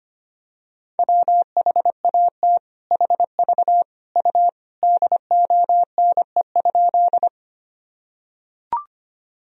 Morse code, heard as a keyed tone, about 25 wpm, 700 Hz. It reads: WHAT HV U DONE?
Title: Sentences from Top 500 Words and Abbreviations 25wpm